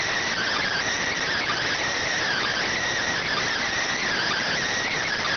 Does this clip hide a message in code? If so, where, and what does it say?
Yes: Emergency Alert System (SAME) tones (header bursts) from 0.3 to 4.5 s, heard as harsh buzzing bursts.